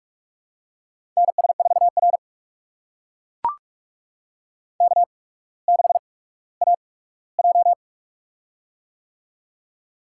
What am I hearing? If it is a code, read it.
Morse code, heard as a keyed tone, about 44 wpm, spaced as for 19 wpm, 700 Hz. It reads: NS4R K6AJ
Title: USA Call Signs - 4 Characters - 8x Character Spacing - 45wpm